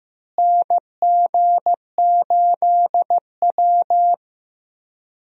Morse code, heard as a keyed tone, about 15 wpm, 700 Hz. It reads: NG8W
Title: USA Call Signs - 4 Characters - Repeated 2x - 15wpm